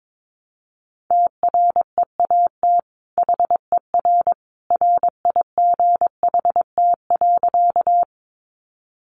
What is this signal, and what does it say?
Morse code, heard as a keyed tone, about 22 wpm, 700 Hz. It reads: TREAT HER RIGHT.
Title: Sentences from Top 1300 Words 22wpm